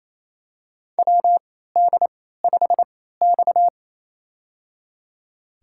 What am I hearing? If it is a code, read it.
Morse code, heard as a keyed tone, about 28 wpm, spaced as for 16 wpm, 700 Hz. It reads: WD5X
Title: USA Call Signs - 4 Characters - 3x Character Spacing - 28wpm